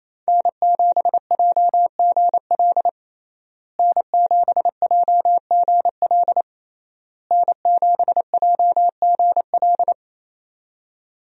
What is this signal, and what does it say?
Morse code, heard as a keyed tone, about 28 wpm, 700 Hz. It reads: N7JGL N7JGL N7JGL
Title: USA Call Signs - 5 Characters - Repeated 3x - 28wpm